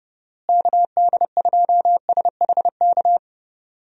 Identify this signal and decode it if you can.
Morse code, heard as a keyed tone, about 29 wpm, 700 Hz. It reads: KD2SHK
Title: USA Call Signs - 6 Characters - Repeated 3x - 30wpm